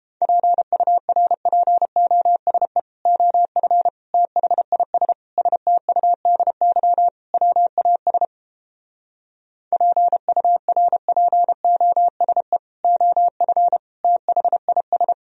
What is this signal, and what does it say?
Morse code, heard as a keyed tone, about 31 wpm, 700 Hz. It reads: PURPOSE OF THIS STUDY WAS PURPOSE OF THIS